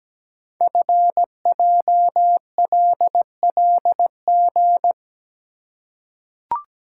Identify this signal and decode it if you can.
Morse code, heard as a keyed tone, about 17 wpm, 700 Hz. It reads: FJLLG
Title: Code Groups - 5 Letters - 17wpm